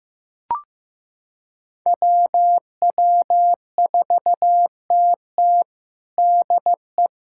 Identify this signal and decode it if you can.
Morse code, heard as a keyed tone, about 16 wpm, 700 Hz. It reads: WW4TT DE
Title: USA Callsign DE Callsign - 5 Characters - 15wpm